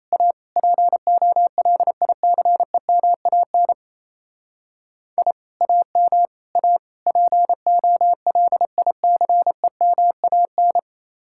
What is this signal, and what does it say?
Morse code, heard as a keyed tone, about 34 wpm, 700 Hz. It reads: A POLICEMAN I AM A POLICEMAN
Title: Speed-Racing Common QSO Phrases - 28wpm